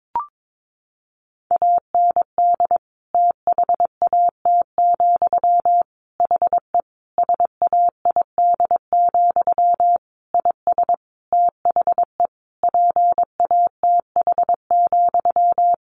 Morse code, heard as a keyed tone, about 22 wpm, 700 Hz. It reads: AND THAT, HE SAID, IS THE PATH,
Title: A Sound Of Thunder 22wpm